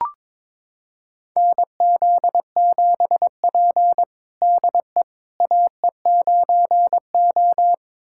Morse code, heard as a keyed tone, about 22 wpm, 700 Hz. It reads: NZ7P DE AE9O
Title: USA Callsign DE Callsign - 4 Characters - 22wpm